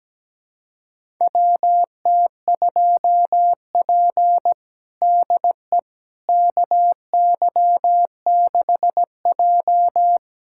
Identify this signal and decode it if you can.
Morse code, heard as a keyed tone, about 17 wpm, 700 Hz. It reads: WT2P DE KY6J
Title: USA Callsign DE Callsign - 4 Characters - 17wpm